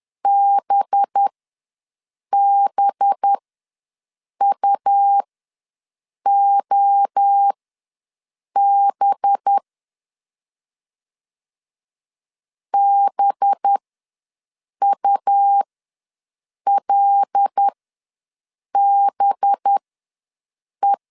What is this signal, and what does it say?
Morse code, heard as a keyed tone, about 10 wpm, spaced as for 6 wpm, 800 Hz. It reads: BBUOB BULBE
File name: LEZIONI / 5 LBVC / 5 LB ws 10wpm 3616